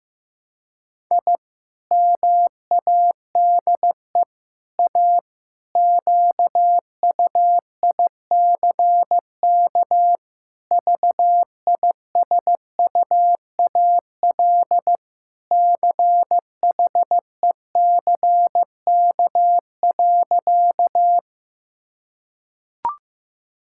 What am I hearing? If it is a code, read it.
Morse code, heard as a keyed tone, about 15 wpm, 700 Hz. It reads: I MADE A QUICK VISUAL CHECK.
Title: Sentences from Top 1000 Words - Review Part 6 - 15wpm